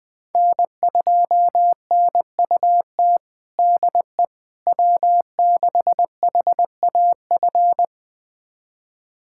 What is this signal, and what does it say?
Morse code, heard as a keyed tone, about 21 wpm, 700 Hz. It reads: N2NUT DE W6HAF
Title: USA Callsign DE Callsign - 5 Characters - 20wpm